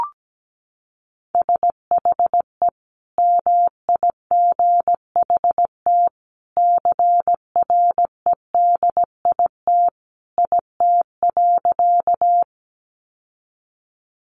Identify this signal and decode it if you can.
Morse code, heard as a keyed tone, about 17 wpm, 700 Hz. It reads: SHE MIGHT CREDIT IT.
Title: Sentences from Top 1000 Words 17wpm